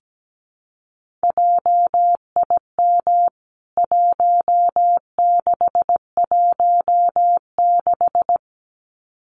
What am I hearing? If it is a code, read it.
Morse code, heard as a keyed tone, about 17 wpm, 700 Hz. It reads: JIM 1616
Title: CWT Exchange 17wpm